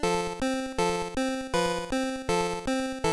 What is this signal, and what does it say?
There is no code.